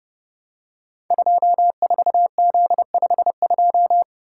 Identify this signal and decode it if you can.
Morse code, heard as a keyed tone, about 30 wpm, 700 Hz. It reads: 24Z52
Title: Code Groups - 5 Letter:Numbers - 30wpm